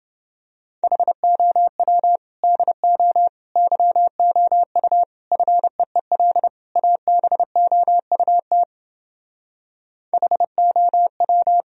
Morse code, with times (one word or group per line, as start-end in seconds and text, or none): 0.8-2.2 HOW
2.4-3.3 DO
3.6-5.0 YOU
5.3-6.5 FEEL
6.8-8.6 ABOUT
10.1-11.6 HOW